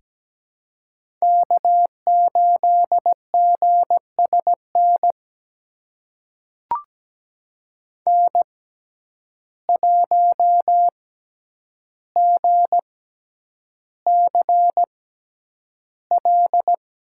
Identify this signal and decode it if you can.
Morse code, heard as a keyed tone, about 17 wpm, spaced as for 9 wpm, 700 Hz. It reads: K8GSN N1GCL
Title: USA Call Signs - 5 Characters - 6x Character Spacing - 17wpm